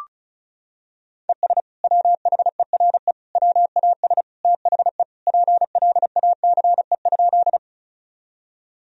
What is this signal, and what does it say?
Morse code, heard as a keyed tone, about 35 wpm, 700 Hz. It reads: ES WHERE WAS THE PLACE?
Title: Sentences from Top 200 Words and Abbreviations 35wpm